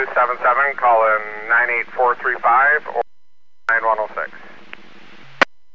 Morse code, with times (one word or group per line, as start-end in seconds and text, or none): none